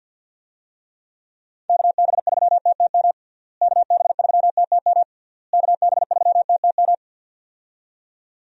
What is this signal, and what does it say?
Morse code, heard as a keyed tone, about 52 wpm, 700 Hz. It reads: KB3TTM KB3TTM KB3TTM
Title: USA Call Signs - 6 Characters - Repeated 3x - 50wpm